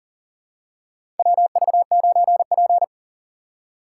Morse code, heard as a keyed tone, about 40 wpm, 700 Hz. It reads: WV9P
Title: USA Call Signs - 4 Characters - 2x Character Spacing - 40wpm